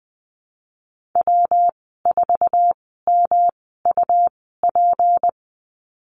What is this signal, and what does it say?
Morse code, heard as a keyed tone, about 20 wpm, spaced as for 14 wpm, 700 Hz. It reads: W4MUP
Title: 186 Lesson 30g - Letter b - Call Signs 20wpm